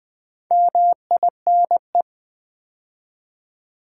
Morse code, heard as a keyed tone, about 20 wpm, 700 Hz. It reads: MINE
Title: Build A Word - 2 Letters At A Time - 4-Letter Words - 20wpm